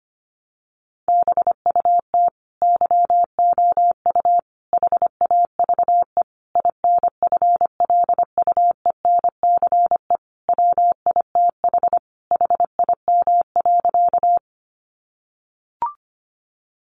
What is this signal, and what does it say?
Morse code, heard as a keyed tone, about 25 wpm, 700 Hz. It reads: BUT YOU HAVE INFLUENCE WITH HIM.
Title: Sentences from Top 1100 Words 25wpm